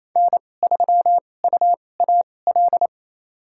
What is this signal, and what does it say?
Morse code, heard as a keyed tone, about 29 wpm, spaced as for 21 wpm, 700 Hz. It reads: N3UAL
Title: USA Call Signs - 5 Characters - 2x Character Spacing - 28wpm